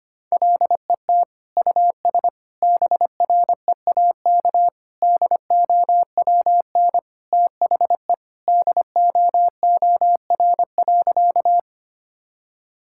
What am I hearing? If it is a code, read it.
Morse code, heard as a keyed tone, about 25 wpm, 700 Hz. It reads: LET US BREAK DOWN THE DOOR.
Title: Sentences from Top 1000 Words - Review Part 6 - 25wpm